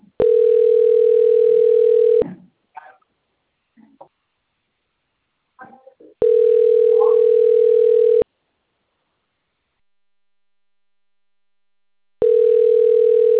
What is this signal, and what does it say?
Ringback tone (North America), 440 + 480 Hz.
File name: rg-1600-unknown- - - .14791